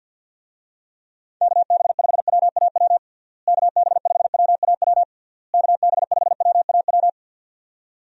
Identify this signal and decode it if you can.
Morse code, heard as a keyed tone, about 52 wpm, 700 Hz. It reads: KB5WAW KB5WAW KB5WAW
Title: USA Call Signs - 6 Characters - Repeated 3x - 50wpm